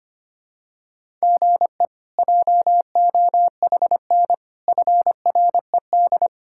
Morse code, heard as a keyed tone, about 25 wpm, 700 Hz. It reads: GE JOHN FRED
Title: K1USN SST QSO Practice - 25wpm